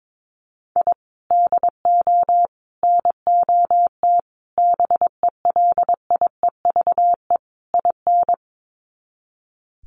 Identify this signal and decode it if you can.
Morse code, heard as a keyed tone, about 22 wpm, 700 Hz. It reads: I DO NOT BELIEVE IN
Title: Sets of 5 Words Encore 22wpm